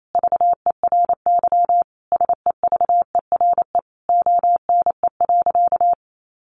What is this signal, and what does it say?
Morse code, heard as a keyed tone, about 28 wpm, 700 Hz. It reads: VERY SEVERE ONE.
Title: Sentences from Top 1400 Words 28wpm